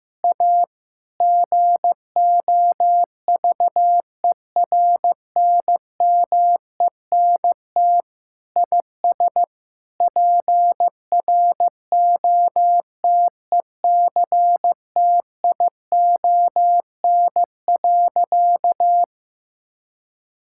Morse code, heard as a keyed tone, about 16 wpm, 700 Hz. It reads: A GOVERNMENT IS PROTECTION.
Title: Sentences from Top 1600 Words 15wpm